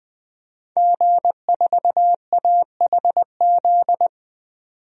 Morse code, heard as a keyed tone, about 20 wpm, 700 Hz. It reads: G4AHZ
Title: DXCC-100 Call Signs - 5 Characters - 20wpm